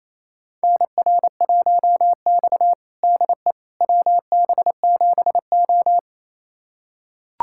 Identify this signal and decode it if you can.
Morse code, heard as a keyed tone, about 29 wpm, 700 Hz. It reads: NR1X DE WB7O
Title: USA Callsign DE Callsign - 4 Characters - 28wpm